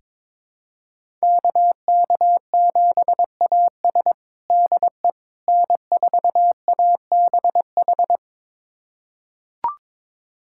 Morse code, heard as a keyed tone, about 22 wpm, 700 Hz. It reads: KK7AS DE N4ABH
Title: USA Callsign DE Callsign - 5 Characters - 22wpm